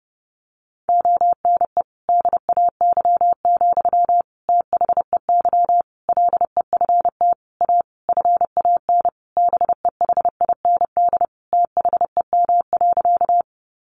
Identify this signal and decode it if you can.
Morse code, heard as a keyed tone, about 28 wpm, 700 Hz. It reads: ONE DAY, THEY LEFT A FAN BEHIND THEM.
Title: Sentences from Top 1300 Words 30wpm